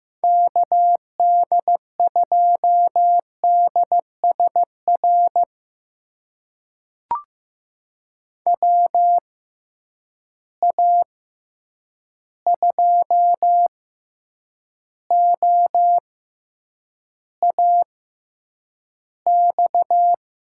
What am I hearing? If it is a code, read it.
Morse code, heard as a keyed tone, about 15 wpm, spaced as for 8 wpm, 700 Hz. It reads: KD2DSR WA2OAX
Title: USA Call Signs - 6 Characters - 6x Character Spacing - 15wpm